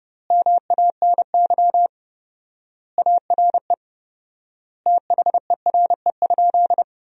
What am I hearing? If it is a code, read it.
Morse code, heard as a keyed tone, about 32 wpm, 700 Hz. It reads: MANY ARE THERE?
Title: 4x Word-Spacing - Sentences from Top 100 Words - 30wpm